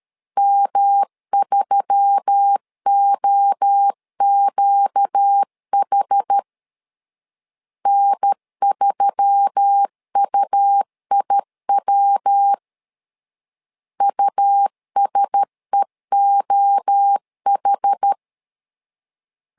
Morse code, heard as a keyed tone, about 13 wpm, 800 Hz. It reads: M3OQH N3UIW USEOH
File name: LEZIONI / 4 Q1S3 12wpm 2023